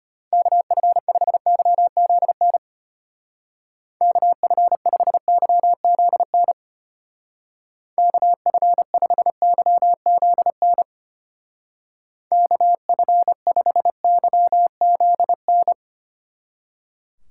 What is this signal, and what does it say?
Morse code, heard as a keyed tone, about 32 wpm, 700 Hz. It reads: KF5YZN KF5YZN KF5YZN KF5YZN